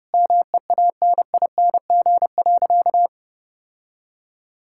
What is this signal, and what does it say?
Morse code, heard as a keyed tone, about 29 wpm, 700 Hz. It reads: MEANING.
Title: Sentences from Top 1000 Words - Review Part 6 - 30wpm